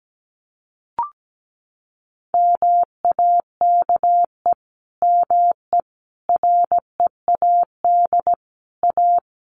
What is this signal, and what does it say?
Morse code, heard as a keyed tone, about 17 wpm, 700 Hz. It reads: MAKE ME READ A
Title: Sentences from Top 1100 Words 17wpm